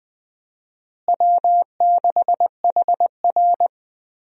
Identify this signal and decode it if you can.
Morse code, heard as a keyed tone, about 20 wpm, 690 Hz. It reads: W6HR
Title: USA Call Signs - 4 Characters - Repeated 3x - 20wpm